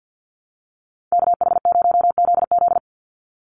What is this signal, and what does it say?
Morse code, heard as a keyed tone, about 50 wpm, 700 Hz. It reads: KH0ZZ
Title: Call Signs Repeated 2x - 50wpm